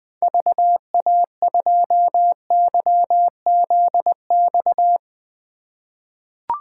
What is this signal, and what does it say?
Morse code, heard as a keyed tone, about 20 wpm, 700 Hz. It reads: VA2YZX
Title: 2023 SOTA Chaser Call Signs - 20wpm